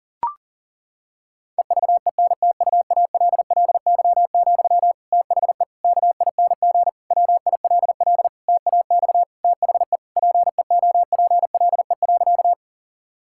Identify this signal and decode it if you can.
Morse code, heard as a keyed tone, about 37 wpm, 700 Hz. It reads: EVENTUALLY, THE KING WILL TAX THE PEOPLE.
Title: Sentences from Top 1000 Words - Review Part 6 - 40wpm